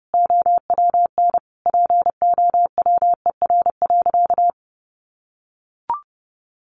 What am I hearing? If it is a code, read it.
Morse code, heard as a keyed tone, about 30 wpm, 700 Hz. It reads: OWN POWER.
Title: Sentences from Top 1200 Words 30wpm